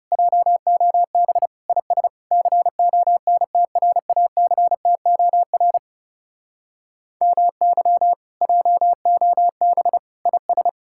Morse code, heard as a keyed tone, about 40 wpm, 700 Hz. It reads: JOB IS CONTRACTOR MY JOB IS